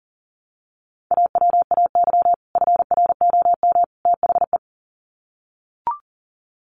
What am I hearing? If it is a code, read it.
Morse code, heard as a keyed tone, about 40 wpm, 700 Hz. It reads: AWAY FROM THE
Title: Sets of 3 Words 40wpm